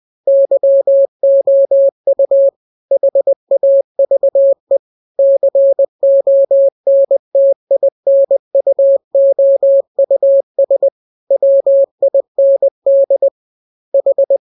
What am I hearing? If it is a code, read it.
Morse code, heard as a keyed tone, about 20 wpm, 550 Hz. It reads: YOU HAVE CONTINUOUS WIND H
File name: jokes / joke 5 20 0000